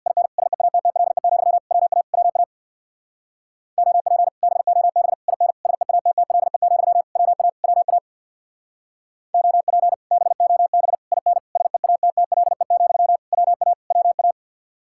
Morse code, heard as a keyed tone, about 62 wpm, 700 Hz. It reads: IN SEATTLE, WA WA OP BOB IN SEATTLE, WA WA OP BOB IN SEATTLE, WA WA
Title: Speed-Racing Common QSO Phrases - 50wpm